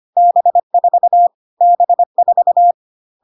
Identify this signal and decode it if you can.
Morse code, heard as a keyed tone, about 25 wpm, 700 Hz. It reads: B4 B4
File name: cw / CW common abbreviations alphabetical repeated 3 times 25 25wpm ik1hge